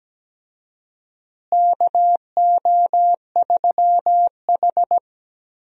Morse code, heard as a keyed tone, about 17 wpm, 700 Hz. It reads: KO3H
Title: USA Call Signs - 4 Characters - Repeated 3x - 17wpm